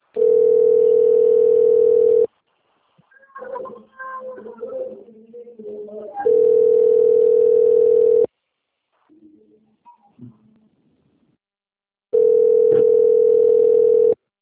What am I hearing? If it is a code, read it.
Ringback tone (North America), 440 + 480 Hz.
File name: rg-1600-unknown- - - .33924